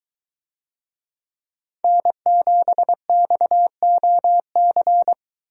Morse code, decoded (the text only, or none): N7XOC